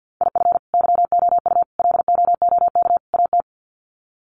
Morse code, heard as a keyed tone, about 50 wpm, 700 Hz. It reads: IF YOU LOOK AT
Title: Sets of 4 Words 50wpm